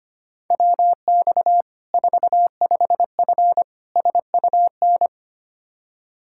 Morse code, heard as a keyed tone, about 25 wpm, 700 Hz. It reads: WX 45F SUN
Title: K1USN SST QSO Practice 25wpm